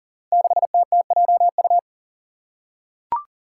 Morse code, heard as a keyed tone, about 39 wpm, 700 Hz. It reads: 6TTJU
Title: Five Challenging Letter-Number Combinations 40wpm